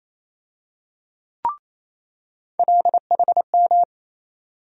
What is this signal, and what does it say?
Morse code, heard as a keyed tone, about 27 wpm, 700 Hz. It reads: LHM
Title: USA Call Sign Postfixes - 3 Letters 28wpm